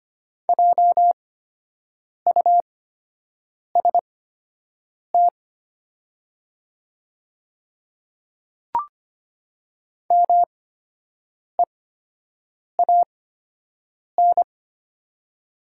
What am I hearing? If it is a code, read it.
Morse code, heard as a keyed tone, about 25 wpm, spaced as for 7 wpm, 700 Hz. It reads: JUST MEAN
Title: 8x Character Spacing - Top 200 Words - 25wpm